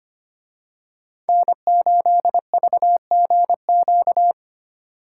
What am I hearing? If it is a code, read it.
Morse code, heard as a keyed tone, about 29 wpm, 700 Hz. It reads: N8VGQ
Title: Speed Racing - USA Call Signs 25wpm